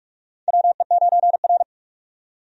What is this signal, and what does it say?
Morse code, heard as a keyed tone, about 42 wpm, 700 Hz. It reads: WE9R
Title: USA Call Signs - 4 Characters - 45wpm